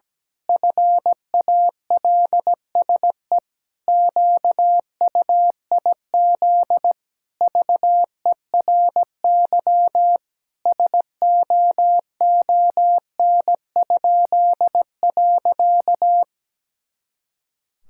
Morse code, heard as a keyed tone, about 17 wpm, 700 Hz. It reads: FALSE QUIZ VERY SOON?.
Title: Pangram Challenge v2 - 17wpm